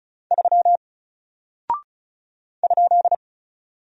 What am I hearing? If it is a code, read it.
Morse code, heard as a keyed tone, about 36 wpm, 700 Hz. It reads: 3 ?
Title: Single Character (Letter:Number:Punctuation:Prosign) Alt 35wpm